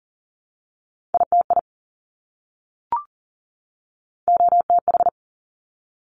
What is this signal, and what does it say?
Morse code, heard as a keyed tone, about 41 wpm, 700 Hz. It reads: ITI OTH
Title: Top Three Letter Combinations in Words 40wpm